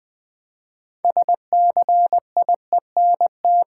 Morse code, heard as a keyed tone, about 20 wpm, 700 Hz. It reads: SCIENT
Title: Build A Word - 3 Letters At A Time - 9-Letter Words - 20wpm